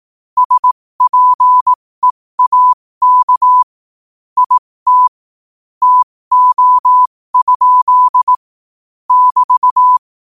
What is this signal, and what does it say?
Morse code, heard as a keyed tone, about 18 wpm, spaced as for 15 wpm, 1000 Hz. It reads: SPEAK IT TO? <BT>
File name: qotdfiles / 15wpm